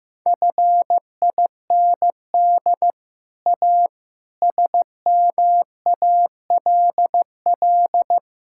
Morse code, heard as a keyed tone, about 15 wpm, 700 Hz. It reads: FIND A SMALL
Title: Sentences from Top 1000 Words - Review Part 5 - 15wpm